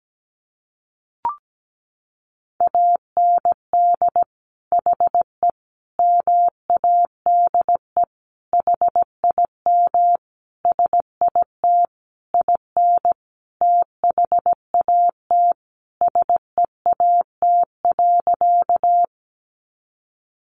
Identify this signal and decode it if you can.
Morse code, heard as a keyed tone, about 17 wpm, 700 Hz. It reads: AND HE MADE HIM SIT IN THAT SEAT.
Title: Sentences from Top 1500 Words 17wpm